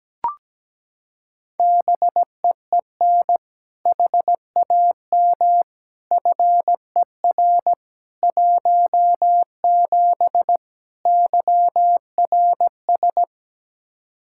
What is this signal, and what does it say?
Morse code, heard as a keyed tone, about 17 wpm, 700 Hz. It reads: BEEN HAM FER 17 YRS
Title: K1USN SST QSO Practice 17wpm